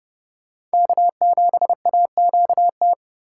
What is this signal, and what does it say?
Morse code, heard as a keyed tone, about 32 wpm, 700 Hz. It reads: K7AQT
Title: USA Call Signs - 5 Characters - 2x Character Spacing - 30wpm